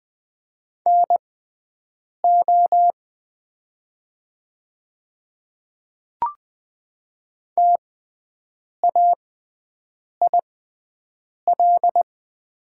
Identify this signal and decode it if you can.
Morse code, heard as a keyed tone, about 20 wpm, spaced as for 7 wpm, 700 Hz. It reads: NO TAIL
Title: 6x Character Spacing - Top 500 Words - 20wpm